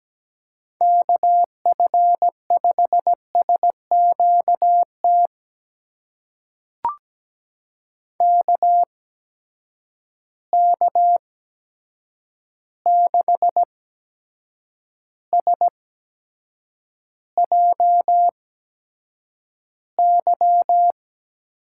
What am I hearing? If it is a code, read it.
Morse code, heard as a keyed tone, about 17 wpm, spaced as for 7 wpm, 700 Hz. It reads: KF5SQT KK6SJY
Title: USA Call Signs - 6 Characters - 8x Character Spacing - 17wpm